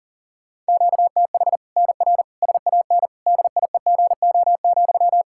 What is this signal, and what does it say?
Morse code, heard as a keyed tone, about 40 wpm, 700 Hz. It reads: QTH NR SAN DIEGO,
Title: Practice QSO - Exchange 1 - Beginning - 40wpm